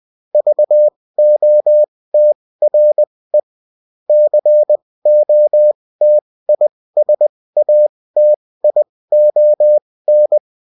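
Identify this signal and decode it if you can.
Morse code, heard as a keyed tone, about 20 wpm, spaced as for 16 wpm, 600 Hz. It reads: VOTRE COTISATION
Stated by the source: Morse code bulletin of a French amateur CW union, sent as text and in CW.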